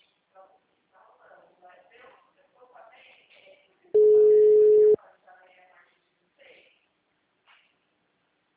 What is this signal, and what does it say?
Ringback tone (Europe, 425 Hz), with speech.
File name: out- -2008- - - .57